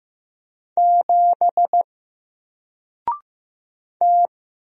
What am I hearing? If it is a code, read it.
Morse code, heard as a keyed tone, about 15 wpm, 700 Hz. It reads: 7 T
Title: Single Letter-Number 15wpm